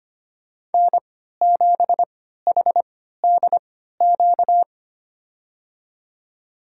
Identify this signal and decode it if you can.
Morse code, heard as a keyed tone, about 25 wpm, spaced as for 14 wpm, 700 Hz. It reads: N7HDQ
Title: USA Call Signs - 5 Characters - 3x Character Spacing - 25wpm